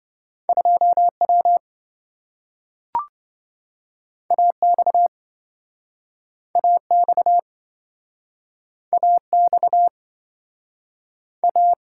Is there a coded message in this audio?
Yes: Morse code from 0.5 to 11.7 s, heard as a keyed tone, about 27 wpm, 700 Hz. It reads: 2W AX AX AX A